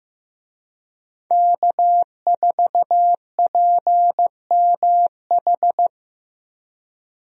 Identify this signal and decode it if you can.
Morse code, heard as a keyed tone, about 15 wpm, 700 Hz. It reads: K4PMH
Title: USA Call Signs - 5 Characters - Repeated 3x - 15wpm